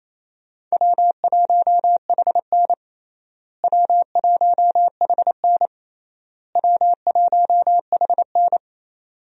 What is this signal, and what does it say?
Morse code, heard as a keyed tone, about 28 wpm, 700 Hz. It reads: W1HN W1HN W1HN
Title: USA Call Signs - 4 Characters - Repeated 3x - 28wpm